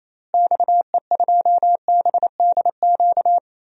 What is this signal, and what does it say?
Morse code, heard as a keyed tone, about 28 wpm, 700 Hz. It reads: XE2BDQ
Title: DXCC-100 Call Signs - 6 Characters - Repeated 2x - 28wpm